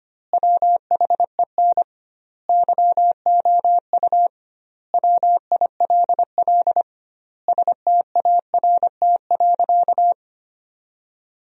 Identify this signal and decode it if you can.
Morse code, heard as a keyed tone, about 25 wpm, 700 Hz. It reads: WHEN YOU WILL START.
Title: 2x Word-Spacing - Sentences from Top 300 Words - 25wpm